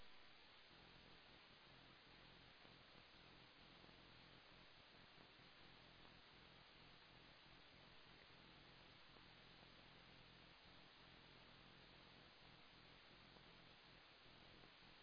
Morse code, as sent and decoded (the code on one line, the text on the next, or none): none